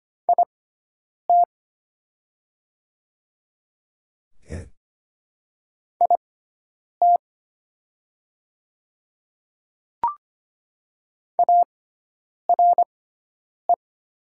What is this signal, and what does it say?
Morse code, heard as a keyed tone, about 25 wpm, spaced as for 9 wpm, 700 Hz. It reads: IT IT ARE
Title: 6x Character Spacing - Top 100 Words - 25wpm